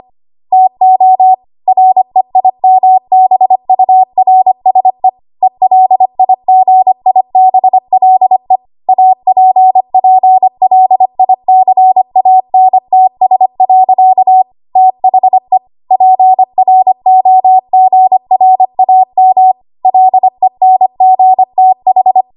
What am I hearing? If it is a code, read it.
Morse code, heard as a keyed tone, about 25 wpm, 750 Hz. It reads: TO REIMBURSE ELIGIBLE APPLICANTS. THE PROGRAM LENGTH
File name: Archive / 25 WPM / 25WPM